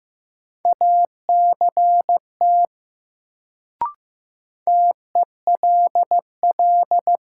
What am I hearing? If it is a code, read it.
Morse code, heard as a keyed tone, about 15 wpm, 700 Hz. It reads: ACT TELL
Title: Sentences from Top 200 Words - New Word Drill - 15wpm